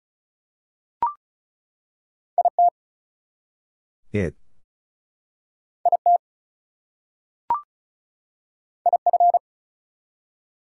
Morse code, heard as a keyed tone, about 34 wpm, 700 Hz. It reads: IT IT IF
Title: Two Letter Words - 35wpm